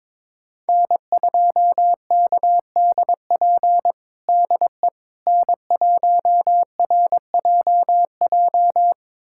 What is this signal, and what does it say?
Morse code, heard as a keyed tone, about 22 wpm, 700 Hz. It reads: N2KDP DE N1RJJ
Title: USA Callsign DE Callsign - 5 Characters - 22wpm